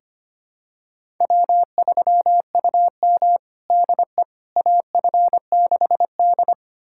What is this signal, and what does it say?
Morse code, heard as a keyed tone, about 25 wpm, 700 Hz. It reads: W3UM DE AF6D
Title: USA Callsign DE Callsign - 4 Characters - 25wpm